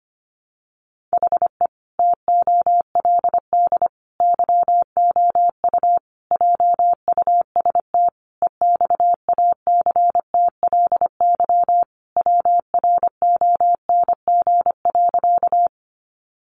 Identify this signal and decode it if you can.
Morse code, heard as a keyed tone, about 25 wpm, 700 Hz. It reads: HE TOLD YOU JUST EXACTLY WRONG.
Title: Sentences from Top 900 Words 25wpm